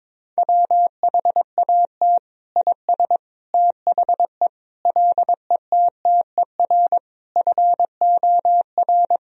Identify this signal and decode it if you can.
Morse code, heard as a keyed tone, about 22 wpm, 700 Hz. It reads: WHAT IS THE LETTER FOR
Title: Sentences from Top 1000 Words - Review Part 2 - 22wpm